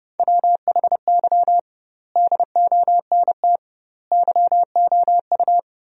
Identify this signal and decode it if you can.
Morse code, heard as a keyed tone, about 32 wpm, 700 Hz. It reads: WHY DONT YOU
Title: 2x Word-Spacing - Sentences from Top 300 Words - 30wpm